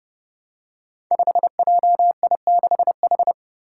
Morse code, heard as a keyed tone, about 30 wpm, 700 Hz. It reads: 5JI6H
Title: Five Challenging Letter-Number Combinations 30wpm